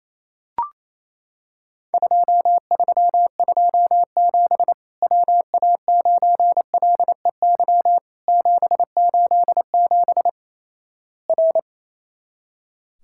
Morse code, heard as a keyed tone, about 27 wpm, 700 Hz. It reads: 2327 WA9LEY 787 R
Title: Worked All Europe DX Contest - QTC Exchange - 28wpm